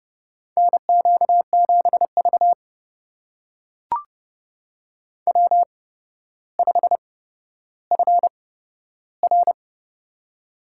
Morse code, heard as a keyed tone, about 29 wpm, spaced as for 13 wpm, 700 Hz. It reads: NQ7V W5FR
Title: USA Call Signs - 4 Characters - 8x Character Spacing - 30wpm